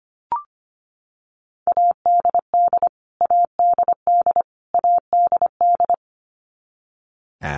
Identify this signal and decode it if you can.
Morse code, heard as a keyed tone, about 25 wpm, 700 Hz. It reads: ADD ADD ADD
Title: Top 200 Words Repeated 25wpm